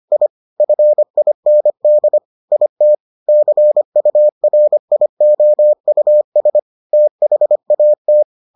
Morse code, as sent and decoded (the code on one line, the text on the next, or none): .. / ..-. .. -. -.. / .. - / -.-. ..- .-. .. --- ..- ... / - .... .- -
I FIND IT CURIOUS THAT